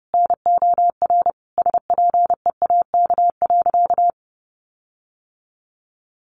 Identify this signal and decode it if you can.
Morse code, heard as a keyed tone, about 30 wpm, 700 Hz. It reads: NOR SPEAK.
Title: Sentences from Top 1500 Words 30wpm